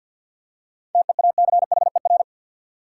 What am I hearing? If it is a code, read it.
Morse code, heard as a keyed tone, about 53 wpm, 700 Hz. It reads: TEACHER
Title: Seven Letter Words - 50wpm